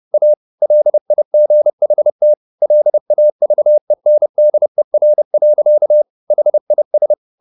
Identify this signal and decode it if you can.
Morse code, heard as a keyed tone, about 30 wpm, 600 Hz. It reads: A LIGHT LAVENDER. HIS
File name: the old man and the sea / the old man and the sea 1 30wpm